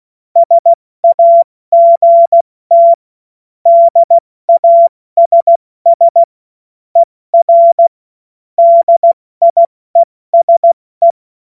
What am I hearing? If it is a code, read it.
Morse code, heard as a keyed tone, about 16 wpm, spaced as for 14 wpm, 680 Hz. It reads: SAGT DASS ER DIESE
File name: Morsekurs / Witze / Witz 30 Januar 2024 70 F80